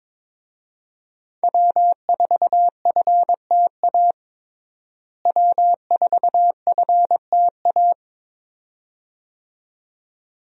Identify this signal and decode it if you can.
Morse code, heard as a keyed tone, about 22 wpm, 700 Hz. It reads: W4FTA W4FTA
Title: USA Call Signs - 5 Characters - Repeated 2x - 22wpm